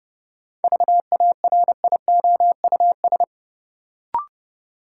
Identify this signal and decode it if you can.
Morse code, heard as a keyed tone, about 32 wpm, 700 Hz. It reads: VARIOUS
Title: Sentences from Top 1000 Words - Review Part 4 - New Word Drill - 30wpm